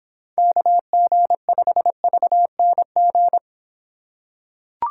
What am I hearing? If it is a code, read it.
Morse code, heard as a keyed tone, about 26 wpm, 700 Hz. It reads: KG5VNG